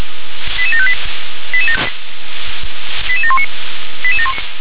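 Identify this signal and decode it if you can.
Five-tone selective call (ZVEI), heard as quick runs of five tones, tones of about 70 ms. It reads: 88758 88719
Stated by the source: ZVEI five-tone selective call received on an SDR.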